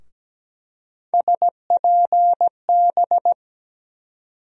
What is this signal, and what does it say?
Morse code, heard as a keyed tone, about 17 wpm, 700 Hz. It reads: SPB
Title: USA Call Sign Postfixes - 3 Letters 17wpm